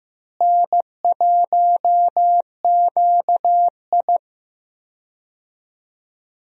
Morse code, heard as a keyed tone, about 15 wpm, 700 Hz. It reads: N1QI
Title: USA Call Signs - 4 Characters - Repeated 3x - 15wpm